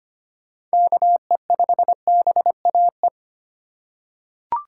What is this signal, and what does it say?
Morse code, heard as a keyed tone, about 25 wpm, 700 Hz. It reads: KE5BAE
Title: USA Call Signs - 6 Characters - 4x Character Spacing - 25wpm